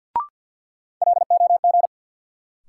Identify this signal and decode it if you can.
Morse code, heard as a keyed tone, about 48 wpm, 700 Hz. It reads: LOG
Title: Sentences from Top 1400 Words - New Word Drill - 50wpm